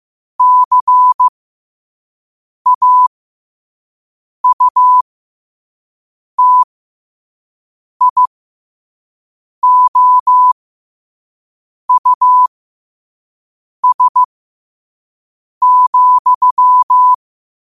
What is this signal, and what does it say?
Morse code, heard as a keyed tone, about 15 wpm, spaced as for 5 wpm, 1000 Hz. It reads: CAUTIOUS,
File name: qotdfiles / 5wpm